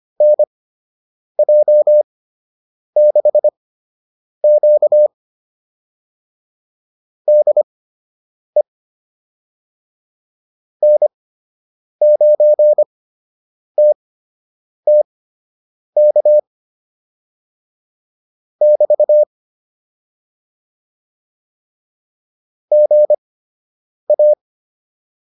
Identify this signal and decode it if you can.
Morse code, heard as a keyed tone, about 25 wpm, spaced as for 8 wpm, 600 Hz. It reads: NJ6Q DE N9TTK <BT> GA